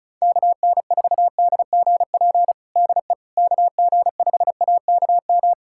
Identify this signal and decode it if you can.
Morse code, heard as a keyed tone, about 36 wpm, 700 Hz. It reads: KN4DGP DE KG5AKM
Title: USA Callsign DE Callsign - 6 Characters - 35wpm